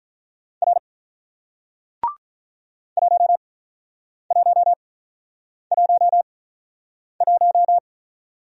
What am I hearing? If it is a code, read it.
Morse code, heard as a keyed tone, about 46 wpm, spaced as for 9 wpm, 700 Hz. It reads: R 1111